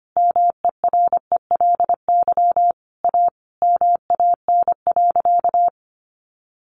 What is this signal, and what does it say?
Morse code, heard as a keyed tone, about 25 wpm, 700 Hz. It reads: MERELY A MAN.
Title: Sentences from Top 1400 Words 25wpm